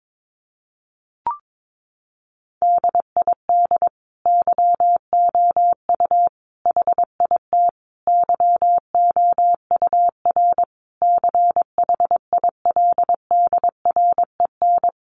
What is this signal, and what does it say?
Morse code, heard as a keyed tone, about 22 wpm, 700 Hz. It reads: DID YOU HIT YOUR CHILDREN
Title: Sentences from Top 1400 Words 22wpm